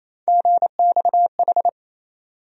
Morse code, heard as a keyed tone, about 28 wpm, 700 Hz. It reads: GXH
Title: USA Call Sign Postfixes - 3 Letters 28wpm